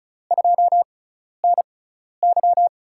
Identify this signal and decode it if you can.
Morse code, heard as a keyed tone, about 34 wpm, spaced as for 12 wpm, 700 Hz. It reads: 2NY
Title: USA Call Signs - 5 Characters - 6x Character Spacing - 35wpm